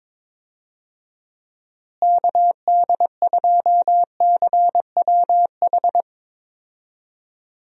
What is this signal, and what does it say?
Morse code, heard as a keyed tone, about 22 wpm, 700 Hz. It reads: KD2CWH
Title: USA Call Signs - 6 Characters - Repeated 3x - 22wpm